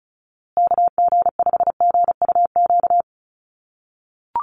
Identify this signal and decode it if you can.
Morse code, heard as a keyed tone, about 34 wpm, 700 Hz. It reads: KG5GUQ
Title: USA Call Signs Repeated 3x 35wpm